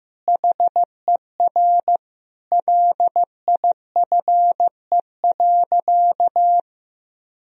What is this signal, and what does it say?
Morse code, heard as a keyed tone, about 15 wpm, 700 Hz. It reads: HER LIFE.
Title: Sentences from Top 1000 Words - Review Part 4 - 15wpm